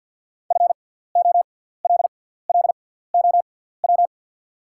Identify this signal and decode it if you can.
Morse code, heard as a keyed tone, about 48 wpm, spaced as for 17 wpm, 700 Hz. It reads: FOLLOW
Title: 6x Character Spacing - Top 200 Words - 50wpm